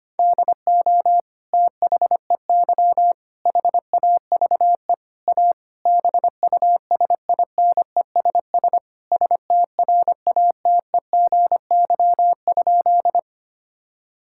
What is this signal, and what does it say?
Morse code, heard as a keyed tone, about 25 wpm, 700 Hz. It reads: DO THEY HAVE A BUSINESS STRATEGY?
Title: Sentences from Top 1000 Words - Review Part 5 - 25wpm